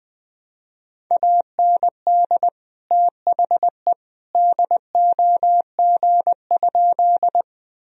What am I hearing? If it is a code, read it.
Morse code, heard as a keyed tone, about 20 wpm, 700 Hz. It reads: AND THE DOG?
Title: Sentences from Top 1000 Words - Review Part 2 - 20wpm